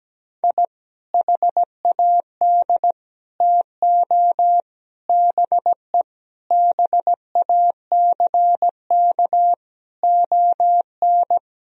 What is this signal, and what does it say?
Morse code, heard as a keyed tone, about 17 wpm, 700 Hz. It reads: I HAD TO BE BACK ON
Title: Sentences from Top 1000 Words - Review Part 4 - 17wpm